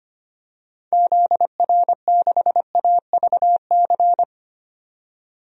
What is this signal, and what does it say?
Morse code, heard as a keyed tone, about 25 wpm, 700 Hz. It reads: ZR6AVC
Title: DXCC-100 Call Signs - 6 Characters - 25wpm